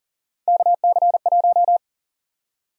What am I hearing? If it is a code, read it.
Morse code, heard as a keyed tone, about 40 wpm, 700 Hz. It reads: KC1
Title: USA Call Sign Prefixes - Repeated 2x - 40wpm